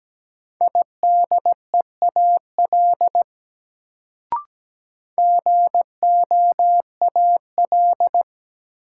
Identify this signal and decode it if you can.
Morse code, heard as a keyed tone, about 17 wpm, 700 Hz. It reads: IDEAL GOAL
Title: Sentences from Top 1400 Words - New Word Drill Alt - 17wpm